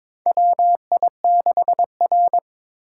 Morse code, heard as a keyed tone, about 21 wpm, 700 Hz. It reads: WI6R
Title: USA Call Signs - 4 Characters - 6x Character Spacing - 22wpm